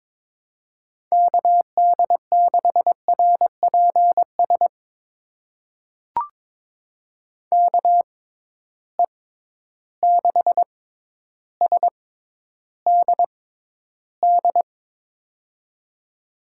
Morse code, heard as a keyed tone, about 22 wpm, spaced as for 11 wpm, 700 Hz. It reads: KD6RPS KE6SDD
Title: USA Call Signs - 6 Characters - 6x Character Spacing - 22wpm